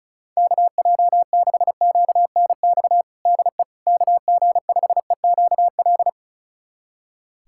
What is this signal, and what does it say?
Morse code, heard as a keyed tone, about 36 wpm, 700 Hz. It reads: KJ6QNX DE KG5EQL
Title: USA Callsign DE Callsign - 6 Characters - 35wpm